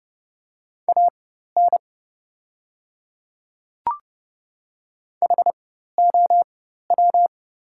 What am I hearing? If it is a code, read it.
Morse code, heard as a keyed tone, about 30 wpm, spaced as for 14 wpm, 700 Hz. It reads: AN HOW
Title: 4x Character Spacing - Top 100 Words - 30wpm